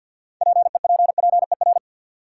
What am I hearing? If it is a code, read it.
Morse code, heard as a keyed tone, about 47 wpm, 700 Hz. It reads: PEPPER
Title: Sets of 5 Words - New Word Drill - 50wpm